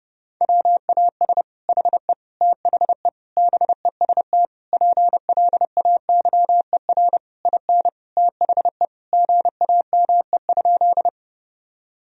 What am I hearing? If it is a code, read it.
Morse code, heard as a keyed tone, about 32 wpm, 700 Hz. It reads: WAS HE THE BEST PLAYER IN THE GAME?
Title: Sentences from Top 1000 Words - Review Part 4 - 30wpm